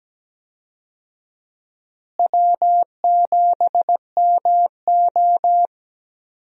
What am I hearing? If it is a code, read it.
Morse code, heard as a keyed tone, about 17 wpm, 700 Hz. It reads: W7MO
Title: USA Call Signs - 4 Characters - Repeated 3x - 17wpm